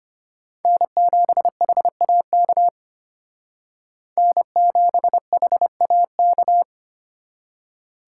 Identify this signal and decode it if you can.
Morse code, heard as a keyed tone, about 33 wpm, 700 Hz. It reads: N7HAK N7HAK